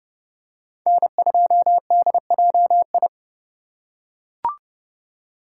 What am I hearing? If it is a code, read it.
Morse code, heard as a keyed tone, about 30 wpm, 700 Hz. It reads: N2DJI